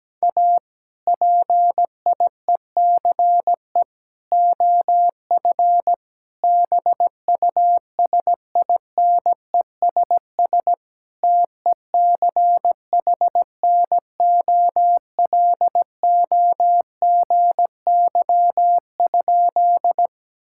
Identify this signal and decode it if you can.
Morse code, heard as a keyed tone, about 17 wpm, 700 Hz. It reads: A PIECE OF BUSINESS TECHNOLOGY?
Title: Sentences from Top 1000 Words - Review Part 5 - 17wpm